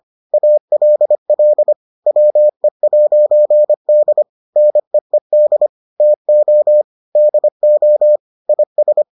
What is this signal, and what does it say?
Morse code, heard as a keyed tone, about 25 wpm, 600 Hz. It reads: ALL WE'D NEED TO DO IS